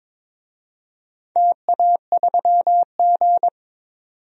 Morse code, heard as a keyed tone, about 21 wpm, 700 Hz. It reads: TA3G
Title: DXCC-100 Call Signs - 4 Characters - 22wpm